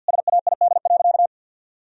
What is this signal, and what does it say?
Morse code, heard as a keyed tone, about 50 wpm, 700 Hz. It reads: SAID.
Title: Sentences from Top 1000 Words - Review Part 3 - 50wpm